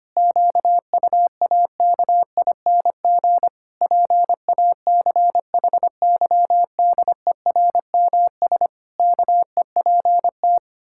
Morse code, heard as a keyed tone, about 25 wpm, 700 Hz. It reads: QUAKING PACHYDERMS KEPT
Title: Pangram Challenge v2 - 25wpm